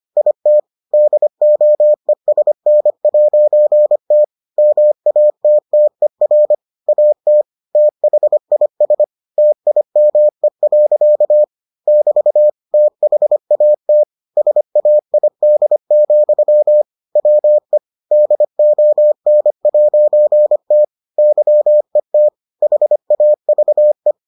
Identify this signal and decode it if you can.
Morse code, heard as a keyed tone, about 25 wpm, 600 Hz. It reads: IT DOESN'T MATTER AT THIS TIME. <BT> THAT SAID, WE DON'T YET HAVE